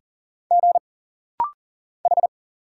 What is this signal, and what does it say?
Morse code, heard as a keyed tone, about 40 wpm, 700 Hz. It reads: G H
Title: Single Letters - Rapid-Fire 40wpm